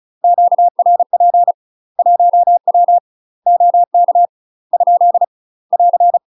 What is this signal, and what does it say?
Morse code, heard as a keyed tone, about 36 wpm, 700 Hz. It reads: QRP 1W OK ? <AR>
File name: UK MASTER QSOs TRAINING / 35WPM / UK MASTER QSOs 0024